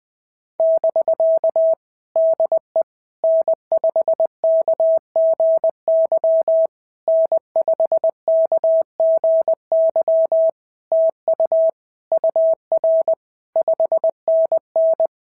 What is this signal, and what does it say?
Morse code, heard as a keyed tone, about 20 wpm, 650 Hz. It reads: <BK> DE N5KGY N5KGY TU UR 5NN
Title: POTA QSO Practice - 20wpm